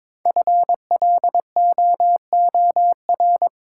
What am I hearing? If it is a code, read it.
Morse code, heard as a keyed tone, about 22 wpm, 700 Hz. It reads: FLOOR
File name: 5 Letter Words - First 3 Letters Given - 22wpm